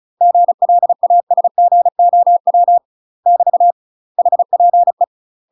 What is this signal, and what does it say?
Morse code, heard as a keyed tone, about 36 wpm, 700 Hz. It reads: GLASGOW <BT> HPE
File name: UK MASTER QSOs TRAINING / 35WPM / UK MASTER QSOs 0038